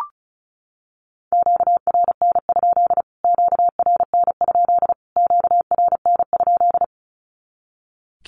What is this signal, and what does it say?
Morse code, heard as a keyed tone, about 34 wpm, 700 Hz. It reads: QRN? QRN? QRN?
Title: Q-Codes Repeated 3x 35wpm